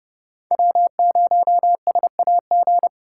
Morse code, heard as a keyed tone, about 38 wpm, 700 Hz. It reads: W0SAG